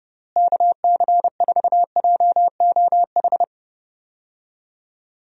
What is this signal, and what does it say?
Morse code, heard as a keyed tone, about 30 wpm, 700 Hz. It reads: KC4JOH
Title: USA Call Signs - 6 Characters - Repeated 2x - 30wpm